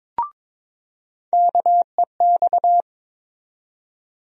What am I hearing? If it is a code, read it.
Morse code, heard as a keyed tone, about 22 wpm, 700 Hz. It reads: KEX
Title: USA Call Sign Postfixes - 3 Letters 22wpm